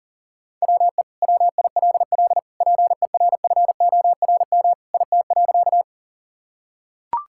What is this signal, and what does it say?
Morse code, heard as a keyed tone, about 40 wpm, 700 Hz. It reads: WE WILL PERFORM IT.
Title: Sentences from Top 1000 Words - Review Part 5 - 40wpm